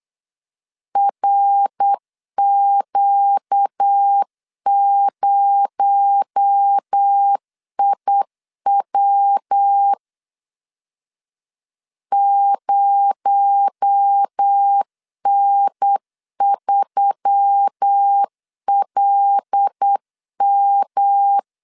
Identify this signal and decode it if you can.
Morse code, heard as a keyed tone, about 8 wpm, 800 Hz. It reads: RQ0IW 0N3LM
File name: LEZIONI / 6 PKX0 / 6 PKX0 8wpm 3341